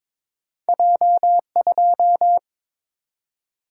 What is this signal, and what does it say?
Morse code, heard as a keyed tone, about 22 wpm, 700 Hz. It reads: J2
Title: Single-Letter Single-Number 22wpm